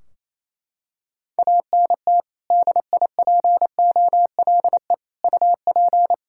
Morse code, heard as a keyed tone, about 28 wpm, 700 Hz. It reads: ANT DIPOLE UP
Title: Practice QSO - Exchange 2 - Info - 28wpm